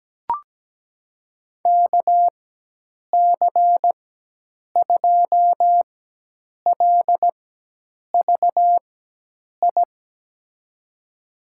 Morse code, heard as a keyed tone, about 17 wpm, spaced as for 8 wpm, 700 Hz. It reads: KC2LVI
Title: USA Call Signs - 6 Characters - 4x Character Spacing - 17wpm